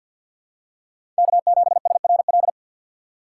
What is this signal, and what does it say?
Morse code, heard as a keyed tone, about 47 wpm, 700 Hz. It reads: K7SRL
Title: USA Call Signs - 5 Characters - 2x Character Spacing - 50wpm